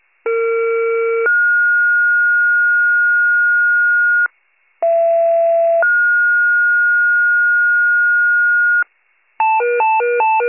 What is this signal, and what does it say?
Two-tone page, heard as a one-second tone, then a three-second tone, about 460 then 1430 Hz, about 670 then 1430 Hz. It is heard from 0.3 to 8.8 s.